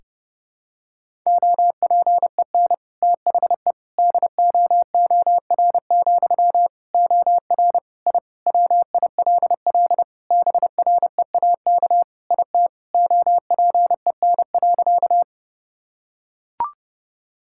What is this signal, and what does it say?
Morse code, heard as a keyed tone, about 32 wpm, 700 Hz. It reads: OPEN THE DOOR, OR I WILL BREAK IT OPEN.
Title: Sentences from Top 1000 Words - Review Part 6 - 30wpm